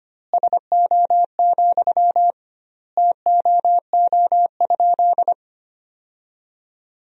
Morse code, heard as a keyed tone, about 25 wpm, 700 Hz. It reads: SO, TOO?
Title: 2x Word-Spacing - Sentences from Top 200 Words - 25wpm